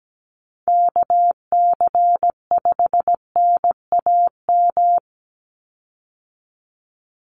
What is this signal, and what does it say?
Morse code, heard as a keyed tone, about 17 wpm, 700 Hz. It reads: KC5NAM
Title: USA Call Signs 17wpm